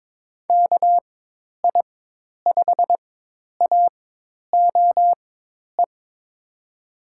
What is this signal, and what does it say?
Morse code, heard as a keyed tone, about 21 wpm, spaced as for 10 wpm, 700 Hz. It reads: KI5AOE
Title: USA Call Signs - 6 Characters - 4x Character Spacing - 22wpm